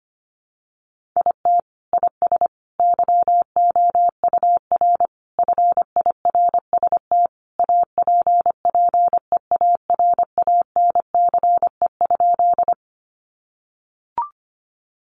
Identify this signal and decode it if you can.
Morse code, heard as a keyed tone, about 25 wpm, 700 Hz. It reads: IT IS YOUR FIRST APPEARANCE?
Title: Sentences from Top 1100 Words 25wpm